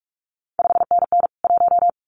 Morse code, heard as a keyed tone, about 46 wpm, 700 Hz. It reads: 5NN 1
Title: CQ WW DX Contest Exchange Encore - 45wpm